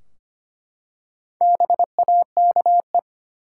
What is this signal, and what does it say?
Morse code, heard as a keyed tone, about 25 wpm, 700 Hz. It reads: BAKE
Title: Four Letter Words - 25wpm